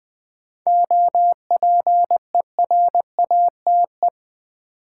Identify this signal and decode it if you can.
Morse code, heard as a keyed tone, about 20 wpm, 700 Hz. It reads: OPERATE
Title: Sentences from Top 1500 Words - New Word Drill - 20wpm